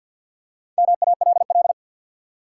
Morse code, heard as a keyed tone, about 47 wpm, 700 Hz. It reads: MALL
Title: Four Letter Words - 50wpm